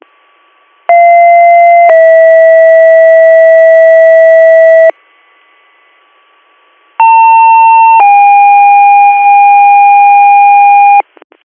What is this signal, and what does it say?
Two-tone page, heard as a one-second tone, then a three-second tone, about 690 then 650 Hz, about 900 then 810 Hz. It is heard from 0.9 to 11.0 s.